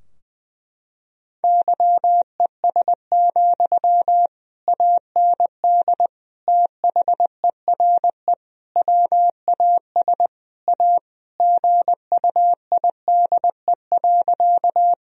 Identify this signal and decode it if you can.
Morse code, heard as a keyed tone, about 21 wpm, 700 Hz. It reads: YES, AND THERE WAS A GUIDE.
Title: Sentences from Top 1000 Words - Review Part 3 - 20wpm